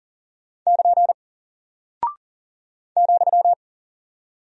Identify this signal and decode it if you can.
Morse code, heard as a keyed tone, about 40 wpm, 700 Hz. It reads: <KN> ,
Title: Single Character (Letter:Number:Punctuation:Prosign) 40wpm